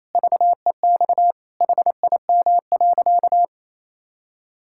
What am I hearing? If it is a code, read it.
Morse code, heard as a keyed tone, about 27 wpm, 700 Hz. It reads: VEX HIM.
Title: Pangram Challenge v2 - 28wpm